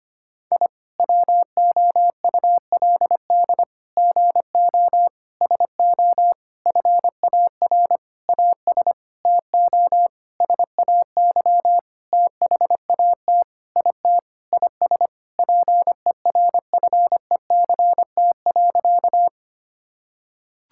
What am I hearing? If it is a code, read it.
Morse code, heard as a keyed tone, about 25 wpm, 700 Hz. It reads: I WOULD GO SO FAR AS TO SAY THAT IT IS PERFECT.
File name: practice / 25wpm / Sentences with 9-Word Phrases - 25wpm